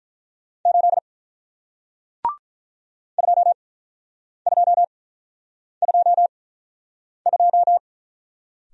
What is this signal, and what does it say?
Morse code, heard as a keyed tone, about 48 wpm, spaced as for 9 wpm, 700 Hz. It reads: 8 2222